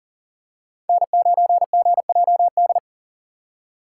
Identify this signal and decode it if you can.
Morse code, heard as a keyed tone, about 40 wpm, 700 Hz. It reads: N9GJD
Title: USA Call Signs - 5 Characters - 3x Character Spacing - 40wpm